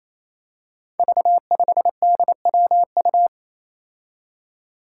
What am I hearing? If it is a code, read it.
Morse code, heard as a keyed tone, about 28 wpm, 700 Hz. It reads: V5DWU
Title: Five Challenging Letter-Number Combinations 28wpm